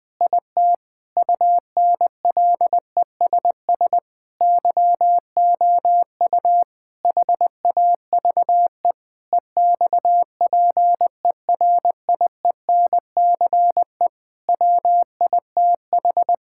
Morse code, heard as a keyed tone, about 20 wpm, 700 Hz. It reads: IT UNLESS YOU HAVE EXPERIENCE WITH
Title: Sentences from Top 1000 Words - Review Part 6 - 20wpm